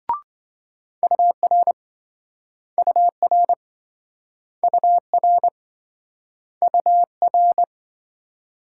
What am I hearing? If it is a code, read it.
Morse code, heard as a keyed tone, about 27 wpm, 700 Hz. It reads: UR UR UR UR